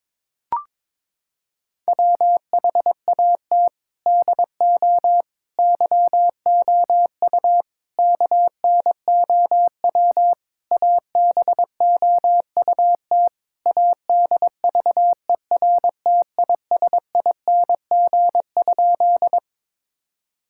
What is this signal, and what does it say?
Morse code, heard as a keyed tone, about 22 wpm, 700 Hz. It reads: WHAT DO YOU KNOW ABOUT ADVERTISING?
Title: Sentences from Top 1000 Words - Review Part 6 - 22wpm